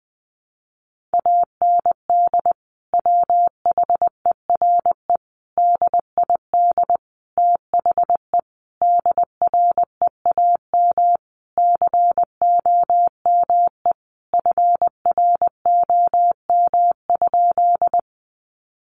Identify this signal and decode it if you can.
Morse code, heard as a keyed tone, about 20 wpm, 700 Hz. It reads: AND WHERE DID THE DREAM COME FROM?
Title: Sentences from Top 1500 Words 20wpm